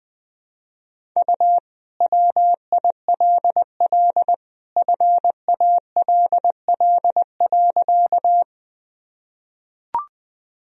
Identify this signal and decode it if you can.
Morse code, heard as a keyed tone, about 20 wpm, 700 Hz. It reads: U WILL FALL.
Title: Sentences from Top 500 Words and Abbreviations 20wpm